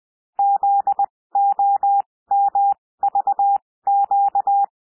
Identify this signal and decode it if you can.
Morse code, heard as a keyed tone, about 20 wpm, spaced as for 16 wpm, 800 Hz. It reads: ZOMVQ
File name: cwmp3 / L1 SOUND LETTERS 20 wpm